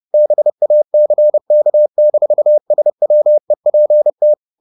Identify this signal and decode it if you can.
Morse code, heard as a keyed tone, about 30 wpm, 600 Hz. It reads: BACK-SWEPT
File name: the old man and the sea / the old man and the sea 1 30wpm